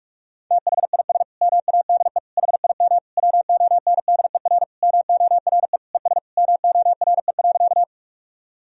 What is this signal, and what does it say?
Morse code, heard as a keyed tone, about 46 wpm, 700 Hz. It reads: THIS MADE HIM WONDER MORE ES MORE.
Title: Sentences from Top 500 Words and Abbreviations 45wpm